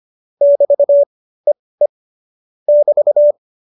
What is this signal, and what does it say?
Morse code, heard as a keyed tone, about 25 wpm, spaced as for 18 wpm, 570 Hz. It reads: <BT> EE <BT>
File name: qso201b 18